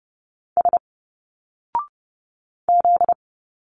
Morse code, heard as a keyed tone, about 30 wpm, 700 Hz. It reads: S Z
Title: Single Letters 30wpm